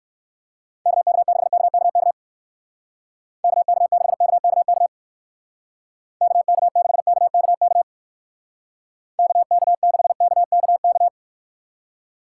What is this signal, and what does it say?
Morse code, heard as a keyed tone, about 54 wpm, 700 Hz. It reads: KK6KKK KK6KKK KK6KKK KK6KKK
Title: Speed Racing - USA 2x3 Call Signs 45wpm